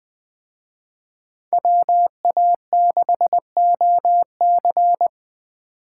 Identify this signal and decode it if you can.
Morse code, heard as a keyed tone, about 20 wpm, 700 Hz. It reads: WA6OC
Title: Speed Racing - USA 2x2 Call Signs 17wpm